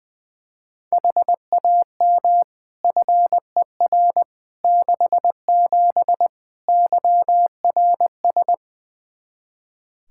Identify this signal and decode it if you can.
Morse code, heard as a keyed tone, about 20 wpm, 700 Hz. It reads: HAM FER 67 YRS
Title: Practice QSO - Exchange 3 - More Info - 20wpm